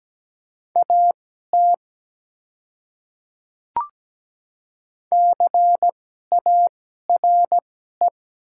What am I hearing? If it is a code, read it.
Morse code, heard as a keyed tone, about 17 wpm, spaced as for 12 wpm, 700 Hz. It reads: AT CARE
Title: 2x Character Spacing - Top 300 Words - 17wpm